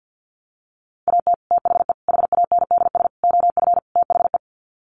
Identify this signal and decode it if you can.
Morse code, heard as a keyed tone, about 50 wpm, 700 Hz. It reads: AT THE HANDS OF THE
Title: Sets of 5 Words Encore2 50wpm